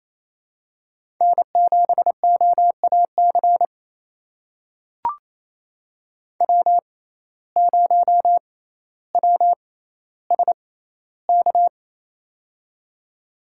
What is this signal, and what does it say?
Morse code, heard as a keyed tone, about 28 wpm, spaced as for 14 wpm, 700 Hz. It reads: N7OAC W0WSK
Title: USA Call Signs - 6x Character Spacing - 28wpm